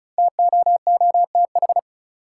Morse code, heard as a keyed tone, about 36 wpm, 700 Hz. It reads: TOOTH
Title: Five Letter Words - 35wpm